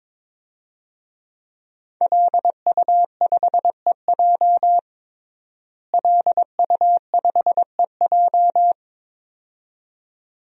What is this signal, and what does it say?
Morse code, heard as a keyed tone, about 22 wpm, 700 Hz. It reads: LU5EJ LU5EJ
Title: DXCC-100 Call Signs - 5 Characters - Repeated 2x - 22wpm